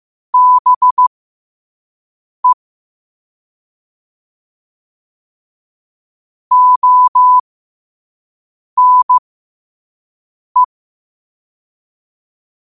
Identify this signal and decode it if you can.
Morse code, heard as a keyed tone, about 15 wpm, spaced as for 5 wpm, 1000 Hz. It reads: BE ONE
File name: qotdfiles / 5wpm